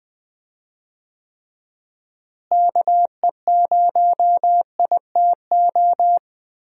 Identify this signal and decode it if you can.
Morse code, heard as a keyed tone, about 20 wpm, 700 Hz. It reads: KE0ITO
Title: USA Call Signs - 6 Characters - Repeated 2x - 20wpm